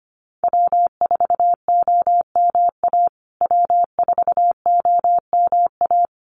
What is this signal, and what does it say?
Morse code, heard as a keyed tone, about 25 wpm, 700 Hz. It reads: W4OMA W4OMA
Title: USA Call Signs Repeated 3x 25wpm